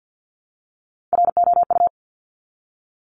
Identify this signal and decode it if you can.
Morse code, heard as a keyed tone, about 50 wpm, 700 Hz. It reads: ROU
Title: Top Three Letter Combinations in Words 50wpm